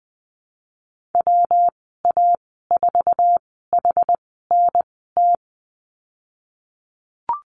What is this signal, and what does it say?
Morse code, heard as a keyed tone, about 20 wpm, spaced as for 14 wpm, 700 Hz. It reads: WA4HNT
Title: 128 Lesson 21g - Number 6 - Call Signs 20wpm